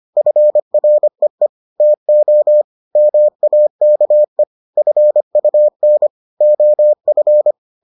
Morse code, heard as a keyed tone, about 25 wpm, 600 Hz. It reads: FREE TO MAKE FUN OF